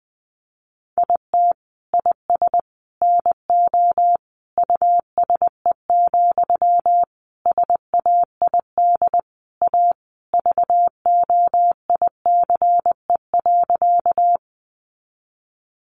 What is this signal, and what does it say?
Morse code, heard as a keyed tone, about 20 wpm, 700 Hz. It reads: IT IS NO USE, SAID A VOICE.
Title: Sentences from Top 500 Words 20wpm